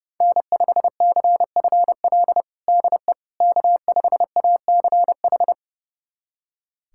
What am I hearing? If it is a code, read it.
Morse code, heard as a keyed tone, about 32 wpm, 700 Hz. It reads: N5CFL DE K5ACH
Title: USA Callsign DE Callsign - 5 Characters - 30wpm